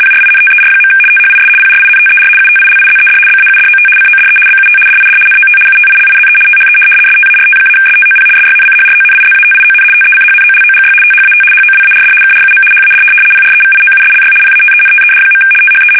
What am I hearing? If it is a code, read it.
RTTY, about 75 baud. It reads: PRINTER. THIS PRINTER UTILIZED BAUDOT CODE BUT DID NOT UTILIZE THE START-STOP SYNCHRONIZATION TECHNOLOGY THAT HOWARD KRUM HAD PREVIOUSLY PATENTED. THE
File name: fsk850-75B / teletype-corp